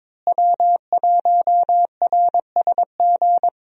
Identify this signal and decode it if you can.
Morse code, heard as a keyed tone, about 22 wpm, 700 Hz. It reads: W1RSG
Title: USA Call Signs - 5 Characters - Repeated 3x - 22wpm